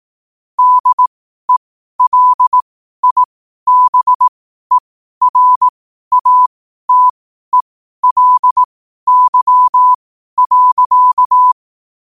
Morse code, heard as a keyed tone, about 18 wpm, spaced as for 12 wpm, 1000 Hz. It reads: DELIBERATELY.
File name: qotdfiles / 12wpm